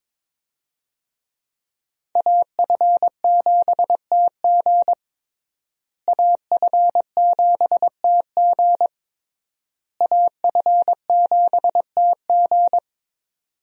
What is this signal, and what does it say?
Morse code, heard as a keyed tone, about 22 wpm, 700 Hz. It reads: AF7TG AF7TG AF7TG
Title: USA Call Signs - 5 Characters - Repeated 3x - 22wpm